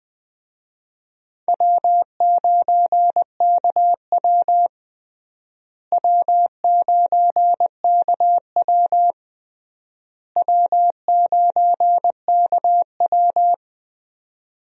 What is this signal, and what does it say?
Morse code, heard as a keyed tone, about 20 wpm, 700 Hz. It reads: W9KW W9KW W9KW
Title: USA Call Signs - 4 Characters - Repeated 3x - 20wpm